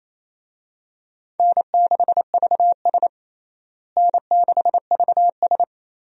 Morse code, heard as a keyed tone, about 28 wpm, 700 Hz. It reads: N6VS N6VS
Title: USA Call Signs - 4 Characters - Repeated 3x - 28wpm